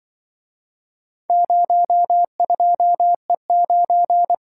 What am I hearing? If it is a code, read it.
Morse code, heard as a keyed tone, about 24 wpm, 700 Hz. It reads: 02E9
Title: Speed-Racing - Four Letter-Number Combinations - 20wpm